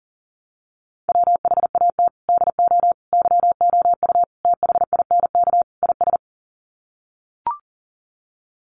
Morse code, heard as a keyed tone, about 40 wpm, 700 Hz. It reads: WHAT DO YOU THINK IS
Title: Sets of 5 Words Encore 40wpm